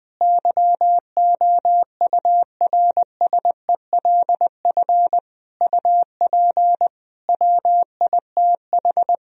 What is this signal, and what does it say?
Morse code, heard as a keyed tone, about 20 wpm, 700 Hz. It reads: YOURSELF UP WITH
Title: Sentences from Top 1000 Words - Review Part 3 - 20wpm